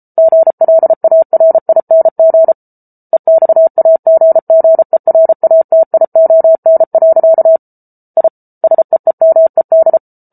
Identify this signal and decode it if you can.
Morse code, heard as a keyed tone, about 34 wpm, 650 Hz. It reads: GLARING EXAGGERATION. I SEEMED